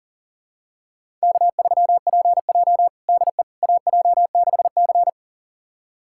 Morse code, heard as a keyed tone, about 40 wpm, 700 Hz. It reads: K3PJ DE AJ6C
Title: USA Callsign DE Callsign - 4 Characters - 40wpm